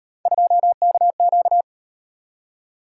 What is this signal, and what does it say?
Morse code, heard as a keyed tone, about 38 wpm, 700 Hz. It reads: 2KQ